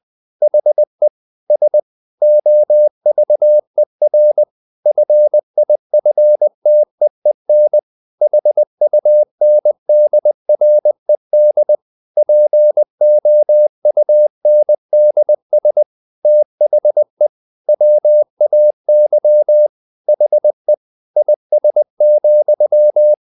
Morse code, heard as a keyed tone, about 20 wpm, 600 Hz. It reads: HE S OVER FIFTEEN HUNDRED POUNDS THE WAY HE IS,
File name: the old man and the sea part 2 20wpm